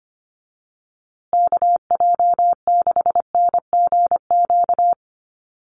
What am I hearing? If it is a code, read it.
Morse code, heard as a keyed tone, about 25 wpm, 700 Hz. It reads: KJ6NGQ
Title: USA Call Signs Repeated 2x 25wpm